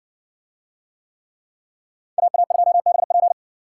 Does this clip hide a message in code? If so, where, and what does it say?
Yes: Morse code from 2.2 to 3.3 s, heard as a keyed tone, about 60 wpm, 700 Hz. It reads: AA2BP